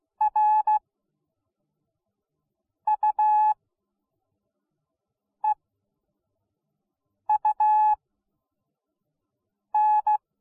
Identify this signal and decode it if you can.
Morse code, heard as a keyed tone, about 16 wpm, spaced as for 4 wpm, 850 Hz. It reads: RUEUN